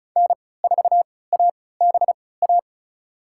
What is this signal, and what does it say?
Morse code, heard as a keyed tone, about 34 wpm, spaced as for 20 wpm, 700 Hz. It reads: N4ABA
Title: USA Call Signs - 5 Characters - 3x Character Spacing - 35wpm